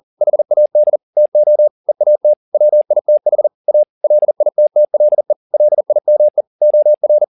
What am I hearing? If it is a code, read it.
Morse code, heard as a keyed tone, about 40 wpm, 600 Hz. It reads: HAD TO EAT WITH A LITTLE LIME OR